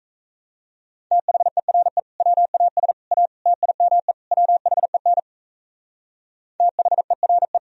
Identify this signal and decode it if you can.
Morse code, heard as a keyed tone, about 36 wpm, 700 Hz. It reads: THERE WAS A TIME WHEN THERE